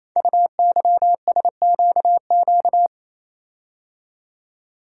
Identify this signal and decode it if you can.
Morse code, heard as a keyed tone, about 27 wpm, 700 Hz. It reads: UYSQQ
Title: Code Groups - 5 Letters - 28wpm